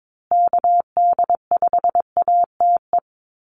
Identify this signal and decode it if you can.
Morse code, heard as a keyed tone, about 22 wpm, 700 Hz. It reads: KD5ATE
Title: USA Call Signs Repeated 2x 22wpm